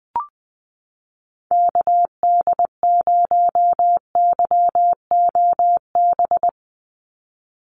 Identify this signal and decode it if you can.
Morse code, heard as a keyed tone, about 20 wpm, 700 Hz. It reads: KD0YOB
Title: NA SOTA Activator Callsigns 20wpm